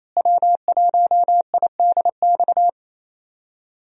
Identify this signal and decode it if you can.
Morse code, heard as a keyed tone, about 27 wpm, 700 Hz. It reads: W1IDX
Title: USA Call Signs - 5 Characters - Repeated 2x - 28wpm